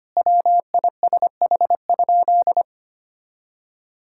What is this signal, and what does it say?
Morse code, heard as a keyed tone, about 25 wpm, 700 Hz. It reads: WISH?
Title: Sentences from Top 1600 Words 25wpm